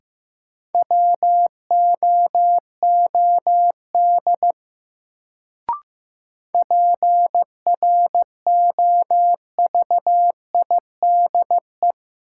Morse code, heard as a keyed tone, about 15 wpm, 700 Hz. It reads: WOOD PROVIDE
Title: Sentences from Top 1100 Words - New Word Drill Alt - 15wpm